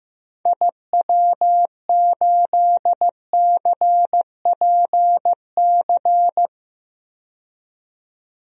Morse code, heard as a keyed tone, about 15 wpm, 700 Hz. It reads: IW8CPC
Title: DXCC-100 Call Signs - 6 Characters - 15wpm